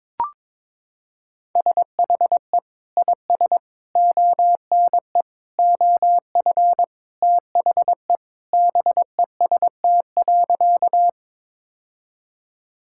Morse code, heard as a keyed tone, about 22 wpm, 700 Hz. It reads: SHE IS ONE OF THE BEST.
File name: practice / 22wpm / Sentences with 2-Word Phrases - 22wpm